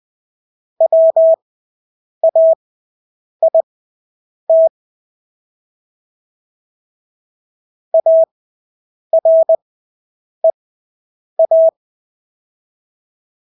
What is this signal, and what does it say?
Morse code, heard as a keyed tone, about 20 wpm, spaced as for 8 wpm, 650 Hz. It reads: WAIT AREA
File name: Engelske ord 20/8 wpm